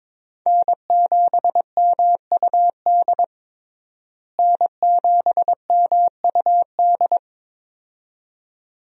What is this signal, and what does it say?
Morse code, heard as a keyed tone, about 22 wpm, 700 Hz. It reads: N7MUD N7MUD
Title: USA Call Signs - 5 Characters - Repeated 2x - 22wpm